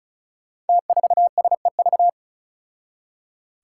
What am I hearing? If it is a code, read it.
Morse code, heard as a keyed tone, about 36 wpm, 700 Hz. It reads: T4SEV
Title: Five Challenging Letter-Number Combinations 35wpm